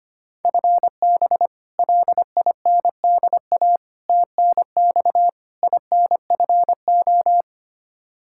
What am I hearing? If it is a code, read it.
Morse code, heard as a keyed tone, about 25 wpm, 700 Hz. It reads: FB LINDA TNX INFO
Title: Practice QSO - Exchange 3 - More Info - 25wpm